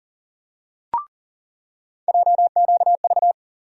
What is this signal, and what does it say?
Morse code, heard as a keyed tone, about 40 wpm, 700 Hz. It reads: JQV
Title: USA Call Sign Postfixes - 3 Letters 40wpm